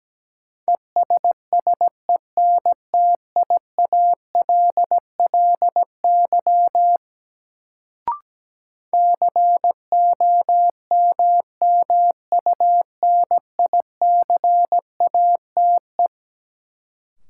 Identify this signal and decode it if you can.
Morse code, heard as a keyed tone, about 17 wpm, 700 Hz. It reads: ESSENTIALLY COMMUNICATE
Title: Sentences from Top 1300 Words - New Word Drill Alt - 17wpm